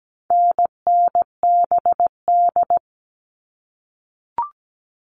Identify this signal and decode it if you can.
Morse code, heard as a keyed tone, about 17 wpm, 700 Hz. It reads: NNBD
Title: All Cut Numbers - 4-Digit 17wpm